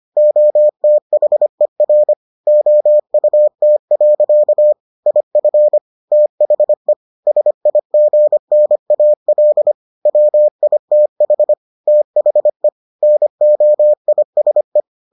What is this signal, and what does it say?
Morse code, heard as a keyed tone, about 25 wpm, 600 Hz. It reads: OTHER OUT. IF THE SIGNAL WITH THE NOISE